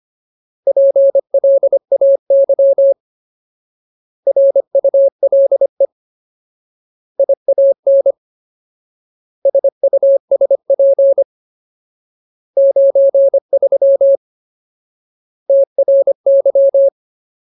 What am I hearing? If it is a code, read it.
Morse code, heard as a keyed tone, about 25 wpm, 550 Hz. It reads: PLAY RULE IAN SUSP 93 TRY
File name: Ultimate Word List / 25WPM Ultimate